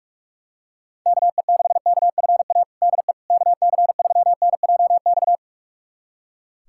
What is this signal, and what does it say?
Morse code, heard as a keyed tone, about 44 wpm, 700 Hz. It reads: KE6KFA DE KC3NJX
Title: USA Callsign DE Callsign - 6 Characters - 45wpm